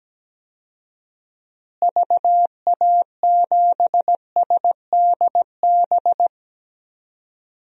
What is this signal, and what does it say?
Morse code, heard as a keyed tone, about 17 wpm, 700 Hz. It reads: VA7SDB
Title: DXCC-100 Call Signs - 6 Characters - Repeated 2x - 17wpm